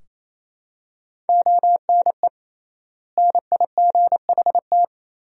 Morse code, heard as a keyed tone, about 28 wpm, 700 Hz. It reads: ONE NIGHT
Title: 3x Word-Spacing - Sentences from Top 300 Words - 28wpm